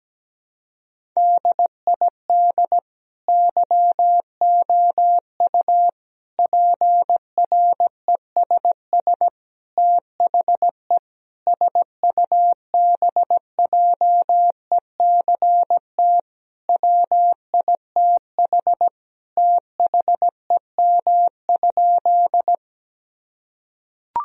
Morse code, heard as a keyed tone, about 17 wpm, 700 Hz. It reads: DID YOU PRESS THE SUBJECT WITH THEM?
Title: Sentences from Top 1000 Words - Review Part 1 - 17wpm